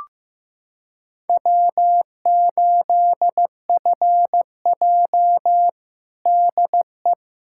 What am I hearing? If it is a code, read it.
Morse code, heard as a keyed tone, about 14 wpm, 700 Hz. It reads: W8FJ DE
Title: Practice QSO - Exchange 4 - Ending - 15wpm